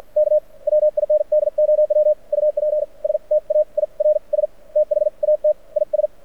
Morse code, heard as a keyed tone, about 56 wpm, spaced as for 43 wpm, 600 Hz. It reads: K WINDOW UPSTAIRS THAT IS
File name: archived / CRSnet